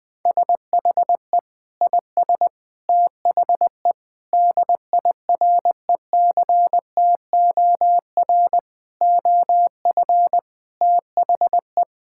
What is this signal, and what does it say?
Morse code, heard as a keyed tone, about 20 wpm, 700 Hz. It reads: SHE IS THE DIRECTOR OF THE
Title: Sentences with 3-Word Phrases - Spoken Beginning - 20wpm